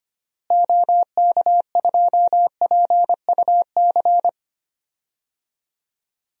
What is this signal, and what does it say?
Morse code, heard as a keyed tone, about 25 wpm, 700 Hz. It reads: OK2PUC
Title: DXCC-100 Call Signs - 6 Characters - Repeated 3x - 25wpm